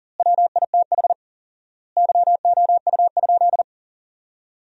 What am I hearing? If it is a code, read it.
Morse code, heard as a keyed tone, about 40 wpm, 700 Hz. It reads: WITH YOU?
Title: 4x Word-Spacing - Sentences from Top 100 Words - 40wpm